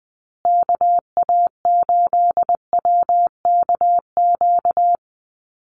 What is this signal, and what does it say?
Morse code, heard as a keyed tone, about 20 wpm, 700 Hz. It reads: KA8WKQ
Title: USA Call Signs Repeated 2x 20wpm